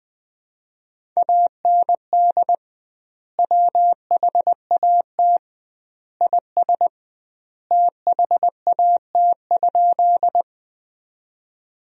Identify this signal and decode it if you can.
Morse code, heard as a keyed tone, about 20 wpm, 700 Hz. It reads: AND WHAT IS THAT?
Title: 2x Word-Spacing - Sentences from Top 100 Words - 20wpm